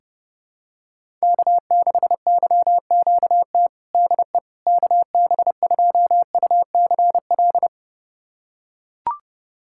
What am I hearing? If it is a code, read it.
Morse code, heard as a keyed tone, about 30 wpm, 700 Hz. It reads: K6YQT DE KB2UCL
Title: USA Callsign DE Callsign 30wpm